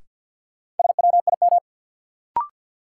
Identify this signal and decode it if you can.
Morse code, heard as a keyed tone, about 53 wpm, 700 Hz. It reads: SWIM
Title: Sentences from Top 1000 Words - Review Part 6 - New Word Drill - 50wpm